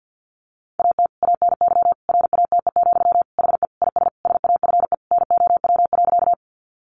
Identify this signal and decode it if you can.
Morse code, heard as a keyed tone, about 50 wpm, 700 Hz. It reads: AT ANY RATE, HE IS SAFE NOW.
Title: Sentences from Top 900 Words 50wpm